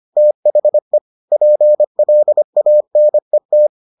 Morse code, heard as a keyed tone, about 25 wpm, 600 Hz. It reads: THE PLANET